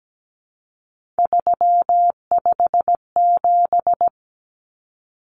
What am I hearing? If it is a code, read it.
Morse code, heard as a keyed tone, about 17 wpm, 700 Hz. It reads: 357
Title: Common Cut Numbers - 3-Digit 17wpm